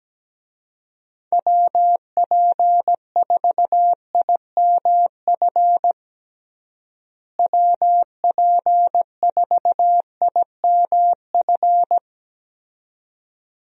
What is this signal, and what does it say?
Morse code, heard as a keyed tone, about 17 wpm, 700 Hz. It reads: WP4IMF WP4IMF
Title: USA Call Signs - 6 Characters - Repeated 2x - 17wpm